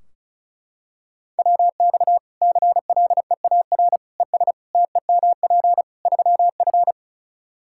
Morse code, heard as a keyed tone, about 34 wpm, 700 Hz. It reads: WX CLEAR ES TEMP 3F
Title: Practice QSO - Exchange 2 - Info - 35wpm